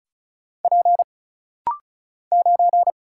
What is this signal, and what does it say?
Morse code, heard as a keyed tone, about 36 wpm, 700 Hz. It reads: P 9
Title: Single Letter-Number Alt - Rapid-Fire 35wpm